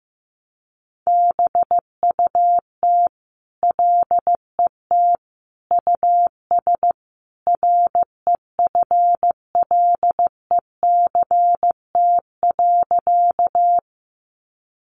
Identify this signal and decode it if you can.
Morse code, heard as a keyed tone, about 15 wpm, 700 Hz. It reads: BUT LET US REFLECT.
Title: Sentences from Top 1500 Words 15wpm